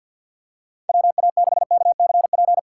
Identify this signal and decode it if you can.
Morse code, heard as a keyed tone, about 47 wpm, 700 Hz. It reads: WA6KCP
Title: USA Call Signs - 6x Character Spacing - 50wpm